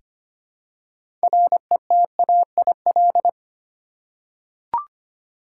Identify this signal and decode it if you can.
Morse code, heard as a keyed tone, about 25 wpm, 700 Hz. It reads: RETAIL
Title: Practice QSO - Exchange 3 - More Info - 25wpm